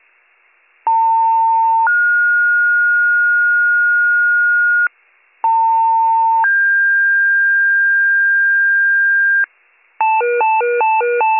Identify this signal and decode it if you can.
Two-tone page, heard as a one-second tone, then a three-second tone, about 900 then 1430 Hz, about 900 then 1600 Hz, with noise. It is heard from 0.9 to 9.4 s.